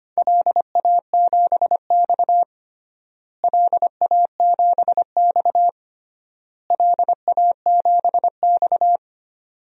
Morse code, heard as a keyed tone, about 25 wpm, 700 Hz. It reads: LA7X LA7X LA7X
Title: DXCC-100 Call Signs - 4 Characters - Repeated 3x - 25wpm